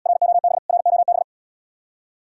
Morse code, heard as a keyed tone, about 80 wpm, 700 Hz. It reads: ROB ROB